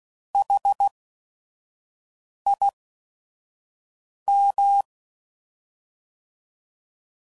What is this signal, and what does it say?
Morse code, heard as a keyed tone, about 16 wpm, spaced as for 5 wpm, 800 Hz. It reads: HIM
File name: Prefixes / By First Letter / S vs Hs